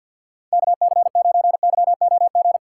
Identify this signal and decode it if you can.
Morse code, heard as a keyed tone, about 47 wpm, 700 Hz. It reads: KC9YOG
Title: USA Call Signs - 6 Characters - 4x Character Spacing - 50wpm